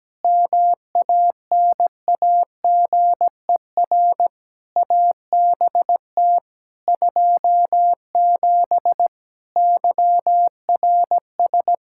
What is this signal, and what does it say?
Morse code, heard as a keyed tone, about 17 wpm, 700 Hz. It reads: MANAGER ABT 27 YRS
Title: Practice QSO - Exchange 3 - More Info - 17wpm